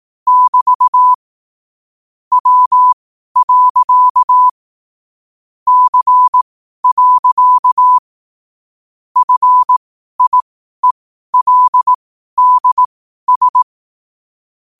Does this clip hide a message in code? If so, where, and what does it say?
Yes: Morse code from 0.3 to 13.6 s, heard as a keyed tone, about 18 wpm, spaced as for 12 wpm, 1000 Hz. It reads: <BT> W. C. FIELDS